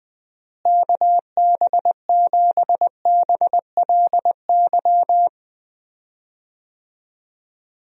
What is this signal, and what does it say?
Morse code, heard as a keyed tone, about 20 wpm, 700 Hz. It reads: KB7BLY
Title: USA Call Signs - 6 Characters - Repeated 3x - 20wpm